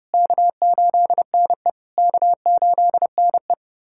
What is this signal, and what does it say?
Morse code, heard as a keyed tone, about 30 wpm, 700 Hz. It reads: K8NE K8NE
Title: SOTA QSO Practice 30wpm